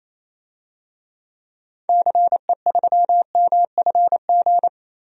Morse code, heard as a keyed tone, about 27 wpm, 700 Hz. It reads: CE3MFG
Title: DXCC-100 Call Signs - 6 Characters - 28wpm